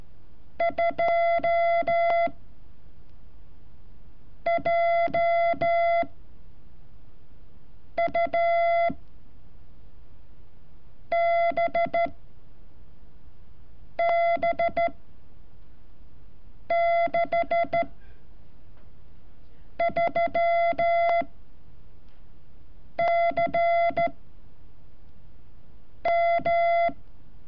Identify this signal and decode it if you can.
Morse code, heard as a keyed tone, about 13 wpm, 680 Hz. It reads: 2 J U B B 6 3 C M
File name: curso / ejercicio 31